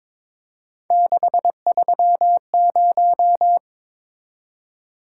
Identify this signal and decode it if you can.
Morse code, heard as a keyed tone, about 27 wpm, 700 Hz. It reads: 630
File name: Speed-Racing - Three Letter-Number Combinations - 22wpm